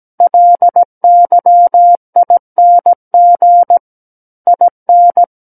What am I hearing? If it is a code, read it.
Morse code, heard as a keyed tone, about 17 wpm, 700 Hz. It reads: LYING IN
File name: Book One / 08-Chapter 8 - 17 WPM